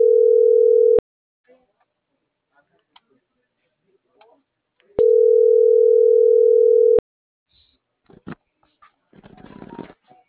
Ringback tone (North America), 440 + 480 Hz.